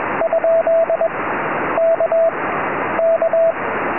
Morse code, heard as a keyed tone, about 21 wpm, 660 Hz, with radio noise. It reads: ? K K
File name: tropo / ly2wr 2m